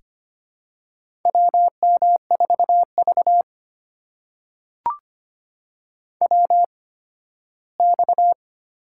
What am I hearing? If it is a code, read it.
Morse code, heard as a keyed tone, about 25 wpm, spaced as for 15 wpm, 700 Hz. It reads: WM4V WX
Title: USA Call Signs - 4 Characters - 8x Character Spacing - 25wpm